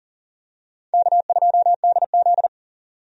Morse code, heard as a keyed tone, about 40 wpm, 700 Hz. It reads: K2DZ
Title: USA Call Signs - 4 Characters - 40wpm